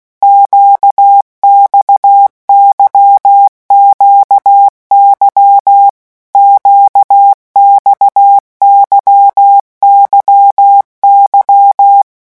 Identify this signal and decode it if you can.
Morse code, heard as a keyed tone, about 16 wpm, 800 Hz. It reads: QXYQY QXYYY
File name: morse mp3 / revision1-vitesse 15 mots